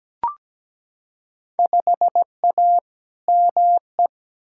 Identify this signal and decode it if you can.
Morse code, heard as a keyed tone, about 17 wpm, 700 Hz. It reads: 5A ME
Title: Field Day Exchanges Encore - 17wpm